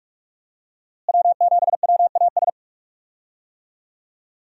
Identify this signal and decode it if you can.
Morse code, heard as a keyed tone, about 46 wpm, 700 Hz. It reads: W7WAS
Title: USA Call Signs - 5 Characters - Repeated 3x - 45wpm